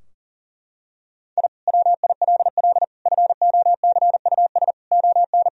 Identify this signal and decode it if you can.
Morse code, heard as a keyed tone, about 40 wpm, 700 Hz. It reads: I WILL FOCUS ON
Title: Sentences from Top 1000 Words - Review Part 4 - 40wpm